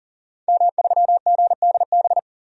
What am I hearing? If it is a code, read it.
Morse code, heard as a keyed tone, about 40 wpm, 700 Hz. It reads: M3GDB
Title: DXCC-100 Call Signs - 5 Characters - 40wpm